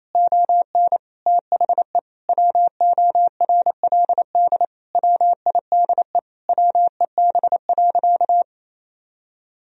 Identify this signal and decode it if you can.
Morse code, heard as a keyed tone, about 27 wpm, 700 Hz. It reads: ON THE WORLD WIDE WEB.
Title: Sentences from Top 1000 Words - Review Part 4 - 28wpm